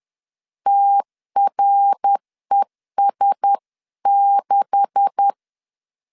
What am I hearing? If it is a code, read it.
Morse code, heard as a keyed tone, about 10 wpm, 800 Hz. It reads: TRES6
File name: CW / DET / 1 9 1500C S B 10wpm 4439